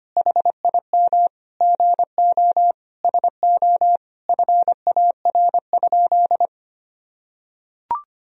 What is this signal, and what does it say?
Morse code, heard as a keyed tone, about 25 wpm, 700 Hz. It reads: HIM GO SO FAR?
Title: Sentences from Top 300 Words and Abbreviations 25wpm